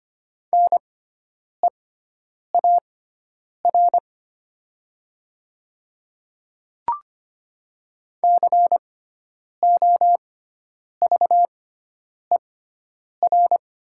Morse code, heard as a keyed tone, about 25 wpm, spaced as for 9 wpm, 690 Hz. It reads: NEAR COVER
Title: 6x Character Spacing - Top 300 Words - 25wpm